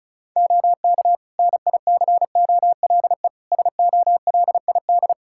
Morse code, heard as a keyed tone, about 36 wpm, 700 Hz. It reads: OK NICOLE SOLID
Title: Practice QSO - Exchange 3 - More Info - 35wpm